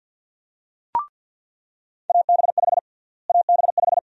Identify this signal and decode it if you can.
Morse code, heard as a keyed tone, about 46 wpm, 700 Hz. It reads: AB5 AB5
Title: USA Call Sign Prefixes - Repeated 2x - 50wpm